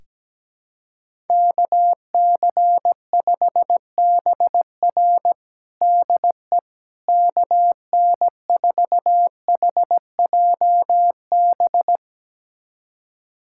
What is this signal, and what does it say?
Morse code, heard as a keyed tone, about 17 wpm, 700 Hz. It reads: KC5BR DE KN4HJB
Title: USA Callsign DE Callsign 17wpm